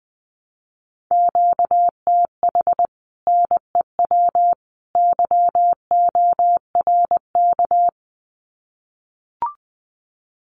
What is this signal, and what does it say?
Morse code, heard as a keyed tone, about 20 wpm, 700 Hz. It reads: QTH NEW YORK
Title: Common QSO Phrases 20wpm